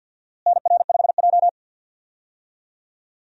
Morse code, heard as a keyed tone, about 52 wpm, 700 Hz. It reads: NR5J
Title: USA Call Signs - 4 Characters - Repeated 2x - 50wpm